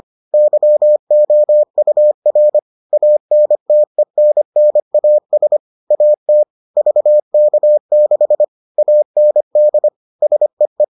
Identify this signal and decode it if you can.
Morse code, heard as a keyed tone, about 25 wpm, 600 Hz. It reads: YOUR ANTENNAS AT VK6 AND SEE